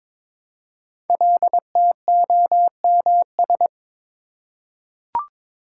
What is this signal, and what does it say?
Morse code, heard as a keyed tone, about 22 wpm, 690 Hz. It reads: LTOMS
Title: Code Groups - 5 Letter:Numbers - 22wpm